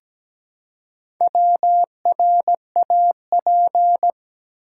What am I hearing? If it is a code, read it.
Morse code, heard as a keyed tone, about 17 wpm, 700 Hz. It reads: WRAP
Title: Four Letter Words - 17wpm